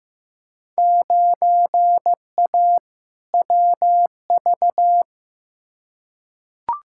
Morse code, heard as a keyed tone, about 15 wpm, 700 Hz. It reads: 9A WV
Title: Field Day Exchanges - 15wpm